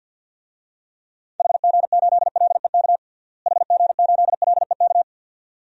Morse code, heard as a keyed tone, about 53 wpm, 700 Hz. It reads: HG8LEK HG8LEK
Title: DXCC-100 Call Signs - 6 Characters - Repeated 2x - 50wpm